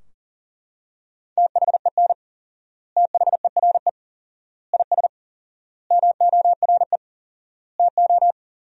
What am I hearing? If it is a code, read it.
Morse code, heard as a keyed tone, about 40 wpm, 700 Hz. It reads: THEN THERE IS MORE TO
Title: 4x Word-Spacing - Sentences from Top 100 Words - 40wpm